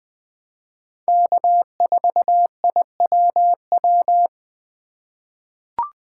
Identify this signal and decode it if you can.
Morse code, heard as a keyed tone, about 21 wpm, 700 Hz. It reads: K4IWW
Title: USA Call Signs - 5 Characters - 3x Character Spacing - 20wpm